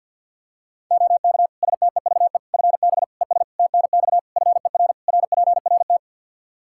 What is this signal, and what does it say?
Morse code, heard as a keyed tone, about 52 wpm, 700 Hz. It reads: OK STEVE FB ES TNX FER RPRT
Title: Practice QSO - Exchange 2 - Info - 50wpm